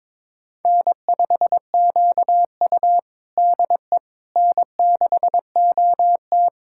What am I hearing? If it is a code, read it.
Morse code, heard as a keyed tone, about 22 wpm, 700 Hz. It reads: N5QU DE N6OT
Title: USA Callsign DE Callsign - 4 Characters - 22wpm